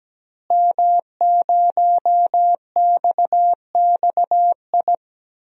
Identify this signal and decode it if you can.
Morse code, heard as a keyed tone, about 17 wpm, 700 Hz. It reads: M0XXI
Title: DXCC-100 Call Signs - 5 Characters - Repeated 3x - 17wpm